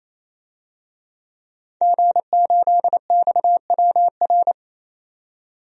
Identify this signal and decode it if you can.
Morse code, heard as a keyed tone, about 28 wpm, 700 Hz. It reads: G8XWR
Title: DXCC-100 Call Signs - 5 Characters - 28wpm